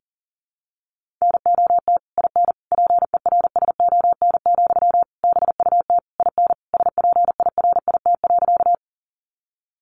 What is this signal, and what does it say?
Morse code, heard as a keyed tone, about 40 wpm, 700 Hz. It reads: NOT IN PERSON, BUT IN SPIRIT.
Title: Sentences from Top 1400 Words 40wpm